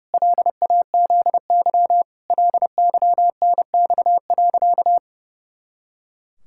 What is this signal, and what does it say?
Morse code, heard as a keyed tone, about 32 wpm, 700 Hz. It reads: LAZY LYNX.
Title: Pangram Challenge v2 - 30wpm